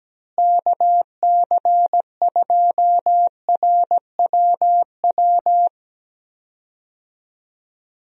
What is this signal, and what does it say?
Morse code, heard as a keyed tone, about 17 wpm, 700 Hz. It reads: KC2RWW
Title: USA Call Signs - 6 Characters - Repeated 2x - 17wpm